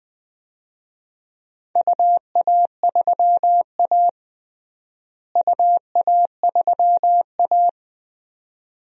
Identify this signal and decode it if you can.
Morse code, heard as a keyed tone, about 20 wpm, 700 Hz. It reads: UA3A UA3A
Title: DXCC-100 Call Signs - 4 Characters - Repeated 2x - 20wpm